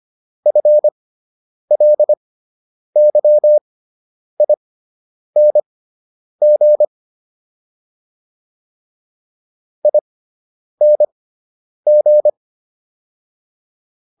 Morse code, heard as a keyed tone, about 25 wpm, spaced as for 9 wpm, 600 Hz. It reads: FLYING ING
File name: ING6 09